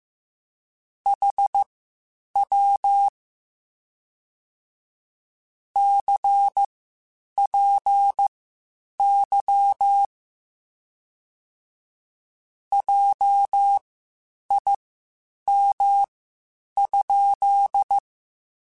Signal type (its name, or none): Morse code